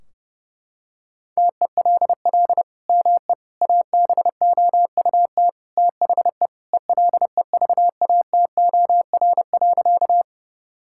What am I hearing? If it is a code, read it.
Morse code, heard as a keyed tone, about 30 wpm, 700 Hz. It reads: TELL ME ABOUT THE ELEVATOR.
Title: Sentences from Top 1600 Words 30wpm